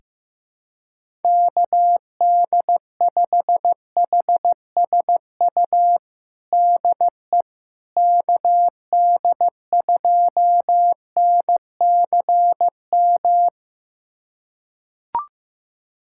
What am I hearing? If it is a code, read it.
Morse code, heard as a keyed tone, about 15 wpm, 700 Hz. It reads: KD5HSU DE KD2NCM
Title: USA Callsign DE Callsign - 6 Characters - 15wpm